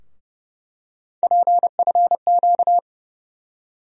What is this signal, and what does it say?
Morse code, heard as a keyed tone, about 30 wpm, 700 Hz. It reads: PFQ
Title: USA Call Sign Postfixes - 3 Letters 30wpm